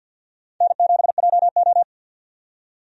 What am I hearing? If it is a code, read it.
Morse code, heard as a keyed tone, about 53 wpm, 700 Hz. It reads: N7JO
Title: USA Call Signs - 4 Characters - 50wpm